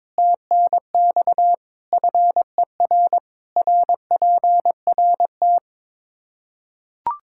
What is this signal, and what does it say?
Morse code, heard as a keyed tone, about 22 wpm, 700 Hz. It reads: TNX FER RPRT
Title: K1USN SST QSO Practice 22wpm